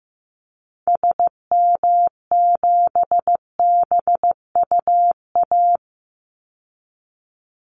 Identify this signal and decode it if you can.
Morse code, heard as a keyed tone, about 15 wpm, 700 Hz. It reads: SM7BUA
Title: Call Signs Repeated 2x - 15wpm